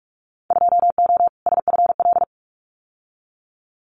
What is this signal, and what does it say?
Morse code, heard as a keyed tone, about 46 wpm, 700 Hz. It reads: 2O SFL
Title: Winter Field Day Exchanges 45wpm